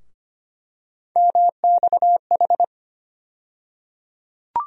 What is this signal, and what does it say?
Morse code, heard as a keyed tone, about 25 wpm, 700 Hz. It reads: MXH
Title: USA Call Sign Postfixes - 3 Letters 25wpm